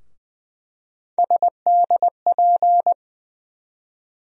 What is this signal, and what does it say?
Morse code, heard as a keyed tone, about 20 wpm, 700 Hz. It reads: SDP
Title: USA Call Sign Postfixes - 3 Letters 20wpm